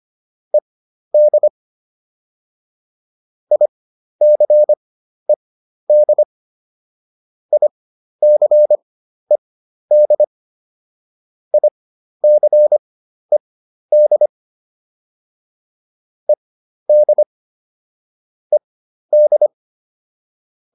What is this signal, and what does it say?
Morse code, heard as a keyed tone, about 25 wpm, spaced as for 12 wpm, 600 Hz. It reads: ED ICED ICED ICED ED ED